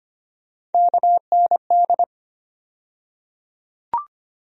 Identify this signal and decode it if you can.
Morse code, heard as a keyed tone, about 25 wpm, 700 Hz. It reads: KND
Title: USA Call Sign Postfixes - 3 Letters 25wpm